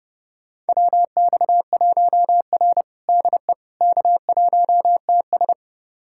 Morse code, heard as a keyed tone, about 30 wpm, 700 Hz. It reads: WX1R DE K1TS
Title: USA Callsign DE Callsign - 4 Characters - 30wpm